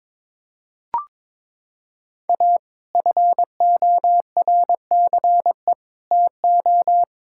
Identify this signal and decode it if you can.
Morse code, heard as a keyed tone, about 22 wpm, 700 Hz. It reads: A FORCE TO
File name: practice / 22wpm / Sets of 6 Words - 22wpm